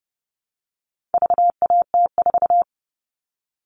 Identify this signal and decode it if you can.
Morse code, heard as a keyed tone, about 30 wpm, 700 Hz. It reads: VAT4
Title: All Cut Numbers - 4-Digit 30wpm